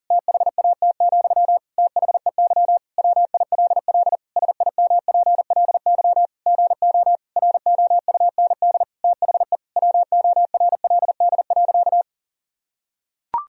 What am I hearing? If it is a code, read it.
Morse code, heard as a keyed tone, about 40 wpm, 700 Hz. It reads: THAT, THEY WILL SIMPLY GO ROUND THE WORLD.
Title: Sentences from Top 1000 Words - Review Part 4 - 40wpm